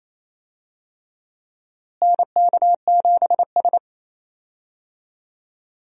Morse code, heard as a keyed tone, about 28 wpm, 700 Hz. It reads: NK7S
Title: USA Call Signs - 4 Characters - Repeated 3x - 28wpm